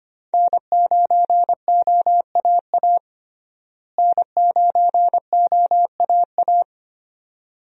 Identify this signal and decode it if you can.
Morse code, heard as a keyed tone, about 25 wpm, 700 Hz. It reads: N9OAA N9OAA
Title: USA Call Signs - 5 Characters - Repeated 2x - 25wpm